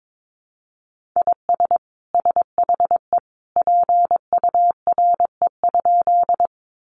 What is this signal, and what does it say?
Morse code, heard as a keyed tone, about 22 wpm, 700 Hz. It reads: IS SHE PURE?
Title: Sentences from Top 1500 Words 22wpm